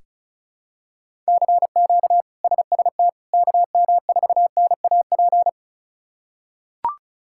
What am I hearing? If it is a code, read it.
Morse code, heard as a keyed tone, about 36 wpm, 700 Hz. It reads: CQ SST KM4NAP
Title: K1USN SST QSO Practice - 35wpm